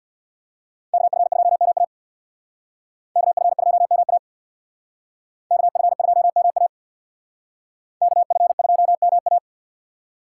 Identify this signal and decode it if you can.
Morse code, heard as a keyed tone, about 63 wpm, 700 Hz. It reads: KF2MA KF2MA KF2MA KF2MA